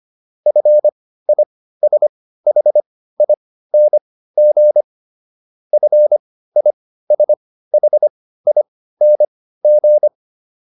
Morse code, heard as a keyed tone, about 25 wpm, spaced as for 15 wpm, 600 Hz. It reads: FISHING FISHING